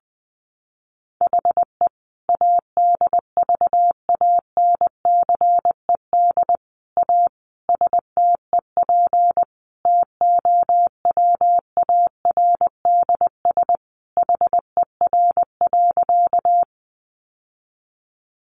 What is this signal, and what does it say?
Morse code, heard as a keyed tone, about 20 wpm, 700 Hz. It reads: HE ADVANCED A STEP TOWARDS HER.
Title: Sentences from Top 1200 Words 20wpm